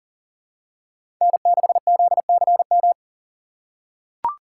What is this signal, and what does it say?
Morse code, heard as a keyed tone, about 40 wpm, 700 Hz. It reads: N6ZCM
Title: USA Call Signs - 5 Characters - 3x Character Spacing - 40wpm